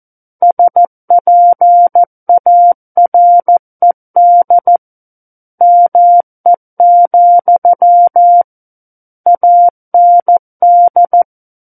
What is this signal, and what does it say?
Morse code, heard as a keyed tone, about 14 wpm, 700 Hz. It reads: SPARED ME, AND
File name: Book One / 05-Chapter 5 - 14 WPM